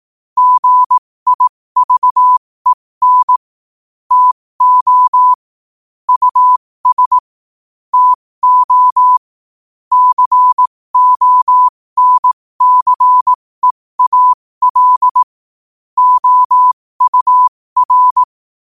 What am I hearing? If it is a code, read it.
Morse code, heard as a keyed tone, about 18 wpm, spaced as for 15 wpm, 1000 Hz. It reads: GIVEN TO US TO CONCEAL OUR
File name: qotdfiles / 15wpm